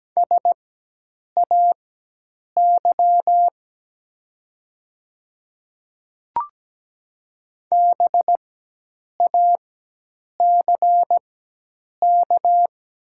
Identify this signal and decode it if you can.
Morse code, heard as a keyed tone, about 17 wpm, spaced as for 8 wpm, 700 Hz. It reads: SAY BACK
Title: 4x Character Spacing - Top 300 Words - 17wpm